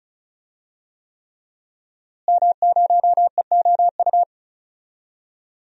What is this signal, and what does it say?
Morse code, heard as a keyed tone, about 34 wpm, 700 Hz. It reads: M0EOU
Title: DXCC-100 Call Signs - 5 Characters - Repeated 3x - 35wpm